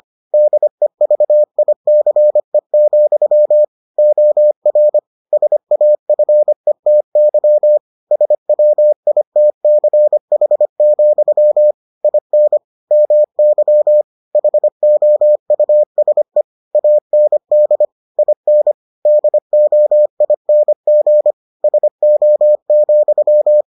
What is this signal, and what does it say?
Morse code, heard as a keyed tone, about 25 wpm, 600 Hz. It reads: DEVICE, OR SAFETY SWITCH, IN MY HOUSE AND IN DOING SO,